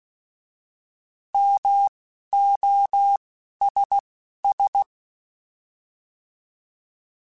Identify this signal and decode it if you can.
Morse code, heard as a keyed tone, about 16 wpm, spaced as for 11 wpm, 800 Hz. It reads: MOSS
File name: Alphabetical / Alphabetical mp3s / Ms @10